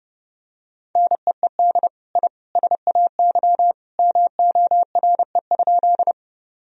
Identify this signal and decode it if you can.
Morse code, heard as a keyed tone, about 30 wpm, 700 Hz. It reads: NEED I SAY MORE?
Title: Sentences from Top 1000 Words - Review Part 1 - 30wpm